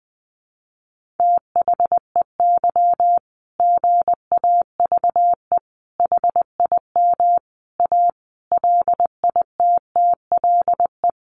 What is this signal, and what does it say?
Morse code, heard as a keyed tone, about 19 wpm, 700 Hz. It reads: THEY GAVE HIM A LITTLE
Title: Sentences from Top 1500 Words 20wpm